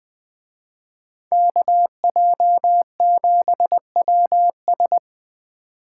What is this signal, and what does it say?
Morse code, heard as a keyed tone, about 21 wpm, 700 Hz. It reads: KJ7WS
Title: Speed Racing - USA Call Signs 17wpm